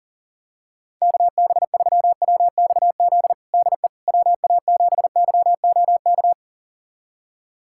Morse code, heard as a keyed tone, about 40 wpm, 700 Hz. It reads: KB3WXZ DE WA7YOK
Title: USA Callsign DE Callsign - 6 Characters - 40wpm